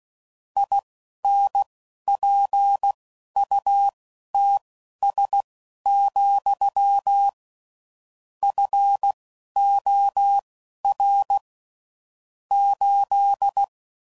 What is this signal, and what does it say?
Morse code, heard as a keyed tone, about 16 wpm, spaced as for 11 wpm, 800 Hz. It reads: INPUTS, FOR 8
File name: CW Code Groups @10WPM/Technical Subjects